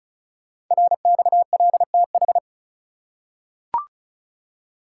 Morse code, heard as a keyed tone, about 36 wpm, 690 Hz. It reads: RXLTH
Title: Code Groups - 5 Letters - 35wpm